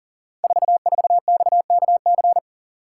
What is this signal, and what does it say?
Morse code, heard as a keyed tone, about 38 wpm, 700 Hz. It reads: 44XKC
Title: Code Groups - 5 Letter:Numbers - 40wpm